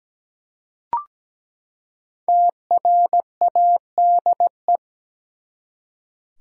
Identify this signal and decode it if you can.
Morse code, heard as a keyed tone, about 17 wpm, 700 Hz. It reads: TRADE
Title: Five Letter Words - 17wpm